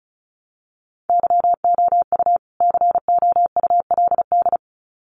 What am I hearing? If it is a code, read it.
Morse code, heard as a keyed tone, about 34 wpm, 700 Hz. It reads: YOU COULD
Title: Sets of 2 Words 35wpm